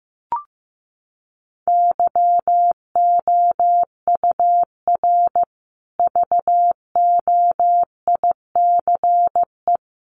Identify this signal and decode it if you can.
Morse code, heard as a keyed tone, about 14 wpm, 700 Hz. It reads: YOUR VOICE
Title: Sentences from Top 500 Words 15wpm